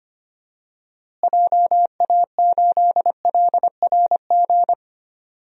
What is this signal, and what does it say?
Morse code, heard as a keyed tone, about 25 wpm, 700 Hz. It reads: JA8LRG
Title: DXCC-100 Call Signs - 6 Characters - Repeated 3x - 25wpm